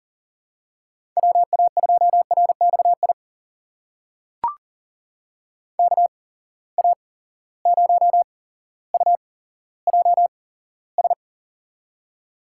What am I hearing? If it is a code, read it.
Morse code, heard as a keyed tone, about 40 wpm, spaced as for 17 wpm, 700 Hz. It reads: WA2RXI KA0UJS
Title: USA Call Signs - 6 Characters - 8x Character Spacing - 40wpm